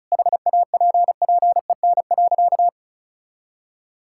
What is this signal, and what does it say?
Morse code, heard as a keyed tone, about 36 wpm, 700 Hz. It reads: HAPPEN.
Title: Sentences from Top 1000 Words - Review Part 6 - 35wpm